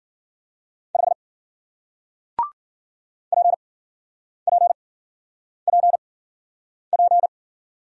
Morse code, heard as a keyed tone, about 50 wpm, spaced as for 9 wpm, 700 Hz. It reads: 5 PPPP
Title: Speed-Racing Single Letter-Number 40wpm